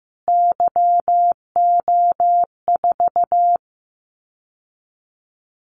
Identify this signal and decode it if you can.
Morse code, heard as a keyed tone, about 15 wpm, 700 Hz. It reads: YO4
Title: Callsign Prefixes for Top 25 DXCC - Repeated 3x - 15wpm